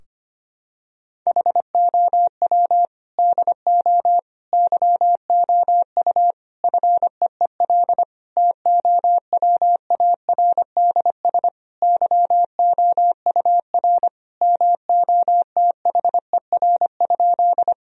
Morse code, heard as a keyed tone, about 25 wpm, 700 Hz. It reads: HOW DO YOU FEEL TOWARDS YOUR MOTHER?
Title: Sentences from Top 1000 Words - Review Part 2 - 25wpm